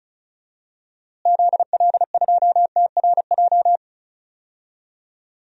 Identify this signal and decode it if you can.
Morse code, heard as a keyed tone, about 36 wpm, 700 Hz. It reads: ZL2TRJ
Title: DXCC-100 Call Signs - 6 Characters - Repeated 2x - 35wpm